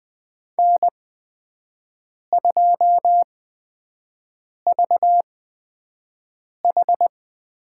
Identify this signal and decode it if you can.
Morse code, heard as a keyed tone, about 20 wpm, spaced as for 5 wpm, 700 Hz. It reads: N2VH
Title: USA Call Signs - 4 Characters - 8x Character Spacing - 20wpm